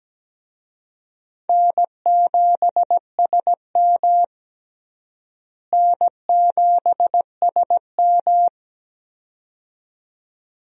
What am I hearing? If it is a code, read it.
Morse code, heard as a keyed tone, about 17 wpm, 700 Hz. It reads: N7SM N7SM
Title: USA Call Signs - 4 Characters - Repeated 2x - 17wpm